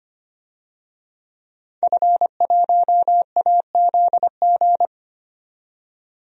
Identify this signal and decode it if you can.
Morse code, heard as a keyed tone, about 25 wpm, 700 Hz. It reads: F1AZG
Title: DXCC-100 Call Signs - 5 Characters - 25wpm